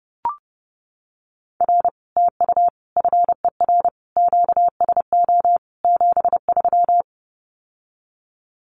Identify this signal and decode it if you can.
Morse code, heard as a keyed tone, about 30 wpm, 700 Hz. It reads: R TU FER QSO 73
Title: Common QSO Phrases Encore 30wpm